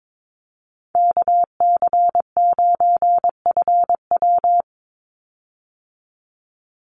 Morse code, heard as a keyed tone, about 22 wpm, 700 Hz. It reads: KC9FW
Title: USA Call Signs 22wpm